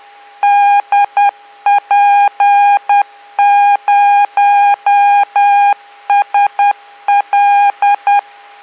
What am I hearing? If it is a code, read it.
Morse code, heard as a keyed tone, about 10 wpm, 810 Hz. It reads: DP0SL